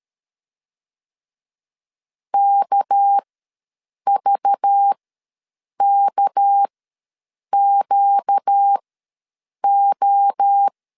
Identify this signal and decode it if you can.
Morse code, heard as a keyed tone, about 13 wpm, spaced as for 7 wpm, 800 Hz. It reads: KVKQO
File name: LEZIONI / 6 PKX0 / 6 PK ws 12wpm 3116